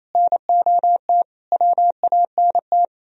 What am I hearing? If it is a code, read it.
Morse code, heard as a keyed tone, about 36 wpm, 700 Hz. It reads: NOT WANT